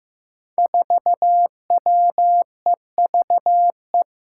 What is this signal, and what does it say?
Morse code, heard as a keyed tone, about 15 wpm, 700 Hz. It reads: 4WEVE
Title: Five Challenging Letter-Number Combinations 15wpm